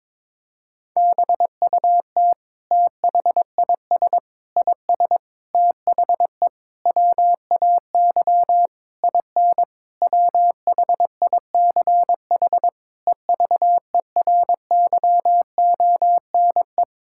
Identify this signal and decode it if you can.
Morse code, heard as a keyed tone, about 22 wpm, 700 Hz. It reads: BUT THIS IS THE WAY IN WHICH EVERYONE
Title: Sentences from Top 1000 Words - Review Part 5 - 22wpm